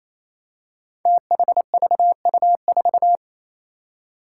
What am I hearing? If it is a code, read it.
Morse code, heard as a keyed tone, about 27 wpm, 700 Hz. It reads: THVU4
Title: Five Challenging Letter-Number Combinations 28wpm